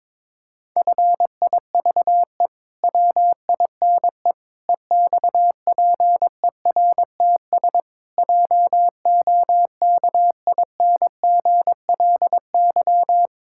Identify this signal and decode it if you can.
Morse code, heard as a keyed tone, about 22 wpm, 700 Hz. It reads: FIVE WINE EXPERTS JOKINGLY
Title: Pangram Challenge v2 - 22wpm